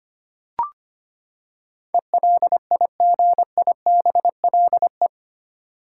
Morse code, heard as a keyed tone, about 25 wpm, 700 Hz. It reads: ELIGIBLE
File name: Eight Letter Words - 25wpm